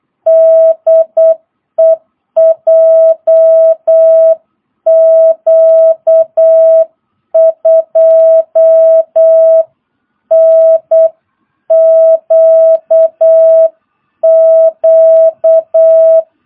Morse code, heard as a keyed tone, about 8 wpm, 650 Hz. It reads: DE JQ2NQQ